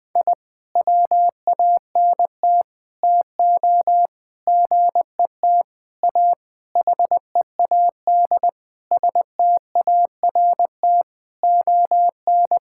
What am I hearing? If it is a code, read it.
Morse code, heard as a keyed tone, about 20 wpm, 700 Hz. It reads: I WANT TO GET A HEAD START ON
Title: Sentences with 9-Word Phrases - 20wpm